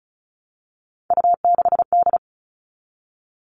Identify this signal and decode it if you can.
Morse code, heard as a keyed tone, about 34 wpm, 700 Hz. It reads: U6D
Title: All Cut Numbers - 3-Digit 35wpm